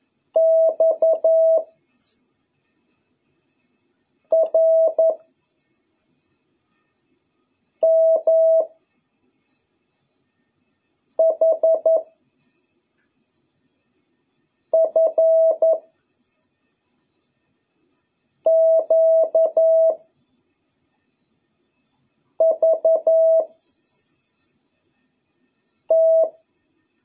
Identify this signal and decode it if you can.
Morse code, heard as a keyed tone, about 11 wpm, 650 Hz. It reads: X R M H F Q V T